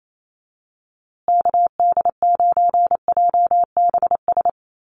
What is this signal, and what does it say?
Morse code, heard as a keyed tone, about 28 wpm, 700 Hz. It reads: KD9JBS
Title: USA Call Signs Repeated 3x 28wpm